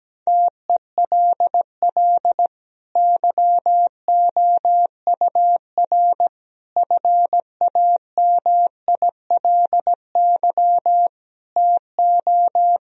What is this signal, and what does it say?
Morse code, heard as a keyed tone, about 17 wpm, 700 Hz. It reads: TELL YOUR FAMILY TO
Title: Sentences from Top 1000 Words - Review Part 2 - 17wpm